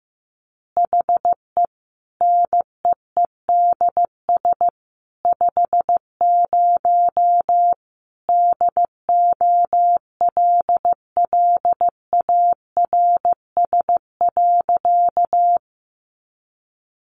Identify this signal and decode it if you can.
Morse code, heard as a keyed tone, about 14 wpm, 700 Hz. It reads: HE NEEDS 50 DOLLARS.
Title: Additional Easy Phrases 15wpm